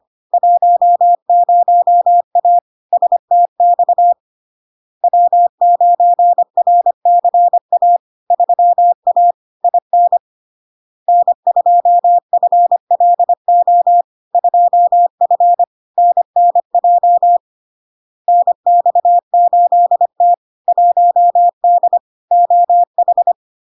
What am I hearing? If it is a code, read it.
Morse code, heard as a keyed tone, about 25 wpm, 700 Hz. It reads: 10A STX W9RCA 3A IN N2FLO 2F NNJ NX8T 1D OH
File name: LARK-FD-2022-Log-in-CW-25wpm-700-Hz